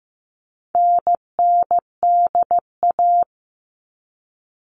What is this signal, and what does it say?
Morse code, heard as a keyed tone, about 15 wpm, 700 Hz. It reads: NNDA
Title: All Cut Numbers - 4-Digit 15wpm